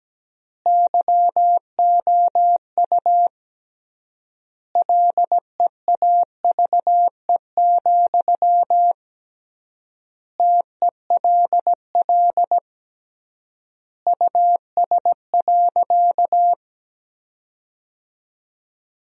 Morse code, heard as a keyed tone, about 17 wpm, 700 Hz. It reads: YOU LEAVE, TELL US.
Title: 3x Word-Spacing - Sentences from Top 400 Words - 17wpm